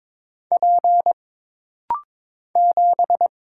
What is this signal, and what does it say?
Morse code, heard as a keyed tone, about 22 wpm, 700 Hz. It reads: P 7
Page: This rapid-fire timing is geared to push your ICR (Instant Character Recognition) skills.